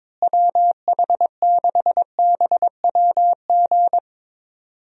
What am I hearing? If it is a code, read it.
Morse code, heard as a keyed tone, about 21 wpm, 700 Hz. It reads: WH6BWG
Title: USA Call Signs - 6 Characters - Repeated 3x - 22wpm